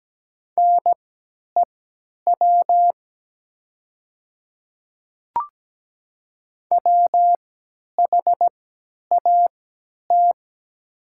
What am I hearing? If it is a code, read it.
Morse code, heard as a keyed tone, about 17 wpm, spaced as for 10 wpm, 700 Hz. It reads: NEW WHAT
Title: 3x Character Spacing - Top 300 Words - 17wpm